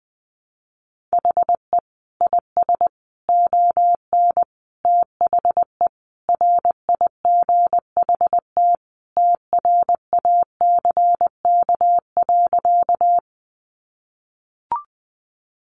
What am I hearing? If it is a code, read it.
Morse code, heard as a keyed tone, about 20 wpm, 700 Hz. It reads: HE IS ON THE RIGHT TRACK.
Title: Sentences from Top 1100 Words 20wpm